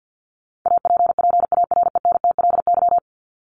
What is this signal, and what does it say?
Morse code, heard as a keyed tone, about 50 wpm, 700 Hz. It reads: APPARENTLY
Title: LY Suffix - Top 100 Words with Base Words 50wpm